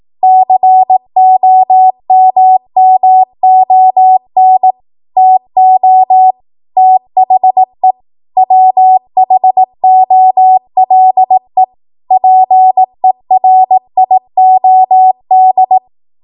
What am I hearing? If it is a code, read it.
Morse code, heard as a keyed tone, about 18 wpm, 750 Hz. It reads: COMMON TO THE WHOLE PERIOD
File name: Archive / 18 WPM / 18WPM